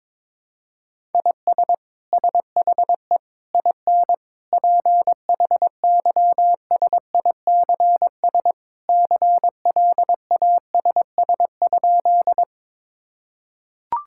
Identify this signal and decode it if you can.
Morse code, heard as a keyed tone, about 22 wpm, 700 Hz. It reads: IS SHE IN PHYSICS CLASS?
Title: Sentences from Top 1000 Words - Review Part 5 - 22wpm